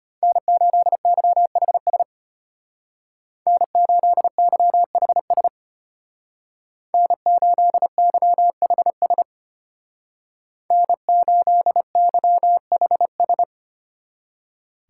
Morse code, heard as a keyed tone, about 31 wpm, 700 Hz. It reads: N8YHS N8YHS N8YHS N8YHS